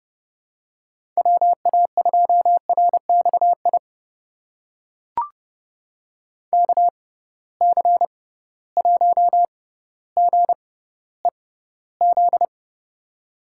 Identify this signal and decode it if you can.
Morse code, heard as a keyed tone, about 30 wpm, spaced as for 15 wpm, 700 Hz. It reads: WA2RXI KC1GEZ
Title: USA Call Signs - 6 Characters - 6x Character Spacing - 30wpm